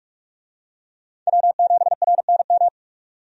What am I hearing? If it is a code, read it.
Morse code, heard as a keyed tone, about 46 wpm, 700 Hz. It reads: W7RNM
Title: USA Call Signs - 5 Characters - 45wpm